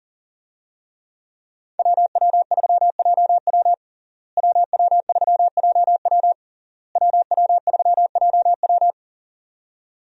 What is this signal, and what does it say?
Morse code, heard as a keyed tone, about 40 wpm, 700 Hz. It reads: WW3JW WW3JW WW3JW
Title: USA Call Signs - 5 Characters - Repeated 3x - 40wpm